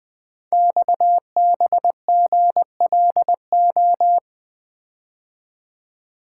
Morse code, heard as a keyed tone, about 20 wpm, 700 Hz. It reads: XBGLO
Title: Code Groups - 5 Letter:Numbers - 20wpm